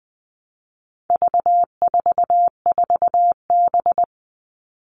Morse code, heard as a keyed tone, about 20 wpm, 700 Hz. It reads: V44B
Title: All Cut Numbers - 4-Digit 20wpm